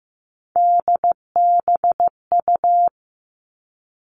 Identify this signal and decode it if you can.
Morse code, heard as a keyed tone, about 15 wpm, 700 Hz. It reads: DBU
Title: All Cut Numbers - 3-Digit 15wpm